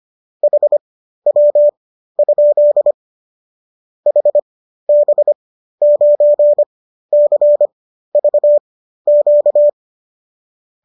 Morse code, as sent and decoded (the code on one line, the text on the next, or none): .... .-- ..--.. / .... -... ----. -.-. ...- --.-
HW? HB9CVQ